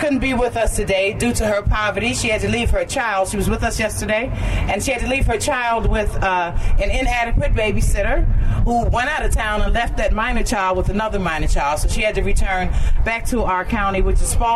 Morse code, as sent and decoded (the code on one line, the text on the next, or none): none